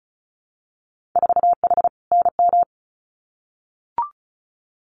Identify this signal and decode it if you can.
Morse code, heard as a keyed tone, about 36 wpm, 700 Hz. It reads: 4H NM
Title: Winter Field Day Exchanges Encore 35wpm